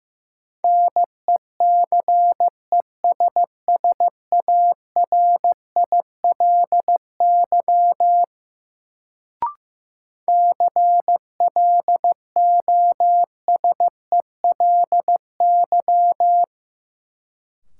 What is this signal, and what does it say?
Morse code, heard as a keyed tone, about 15 wpm, 700 Hz. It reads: NECESSARILY CLOSELY
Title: Sentences from Top 1400 Words - New Word Drill - 15wpm